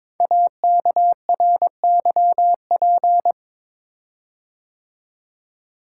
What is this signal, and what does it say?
Morse code, heard as a keyed tone, about 22 wpm, 700 Hz. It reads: AKRYP
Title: Code Groups - 5 Letters - 22wpm